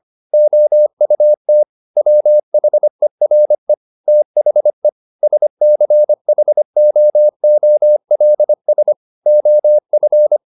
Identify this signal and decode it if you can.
Morse code, heard as a keyed tone, about 25 wpm, 600 Hz. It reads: OUT WHERE THE SCHOOLS OF
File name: the old man and the sea part 1 25wpm